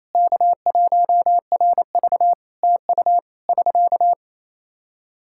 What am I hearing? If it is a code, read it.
Morse code, heard as a keyed tone, about 28 wpm, 700 Hz. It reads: K1RV TU <SK>
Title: K1USN SST QSO Practice 28wpm